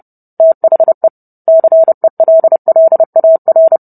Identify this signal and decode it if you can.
Morse code, heard as a keyed tone, about 30 wpm, 650 Hz. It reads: THE CELLAR